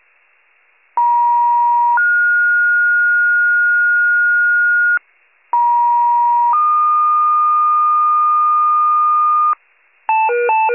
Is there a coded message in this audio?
Yes: two-tone page from 1.0 to 9.5 s, heard as a one-second tone, then a three-second tone, about 950 then 1430 Hz, about 950 then 1220 Hz, with noise.